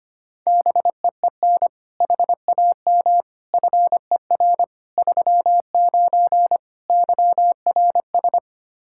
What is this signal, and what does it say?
Morse code, heard as a keyed tone, about 25 wpm, 700 Hz. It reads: BEEN HAM FER 39 YRS
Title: Practice QSO - Exchange 3 - More Info - 25wpm